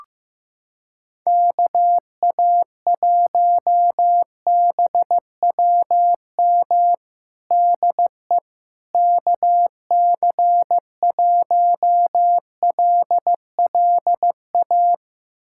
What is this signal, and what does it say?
Morse code, heard as a keyed tone, about 15 wpm, 700 Hz. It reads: KA1BWM DE KC1LLA
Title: USA Callsign DE Callsign 15wpm